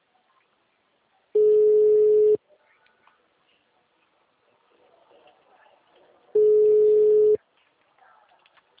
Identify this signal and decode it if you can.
Ringback tone (Europe, 425 Hz).